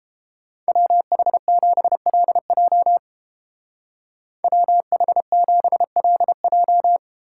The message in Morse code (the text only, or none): WH7LJ WH7LJ